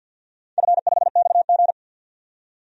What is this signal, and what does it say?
Morse code, heard as a keyed tone, about 50 wpm, 700 Hz. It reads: F5XG
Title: DXCC-100 Call Signs - 4 Characters - 50wpm